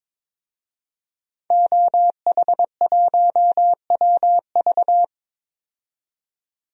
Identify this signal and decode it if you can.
Morse code, heard as a keyed tone, about 22 wpm, 700 Hz. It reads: OH1WV
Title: DXCC-100 Call Signs - 5 Characters - 22wpm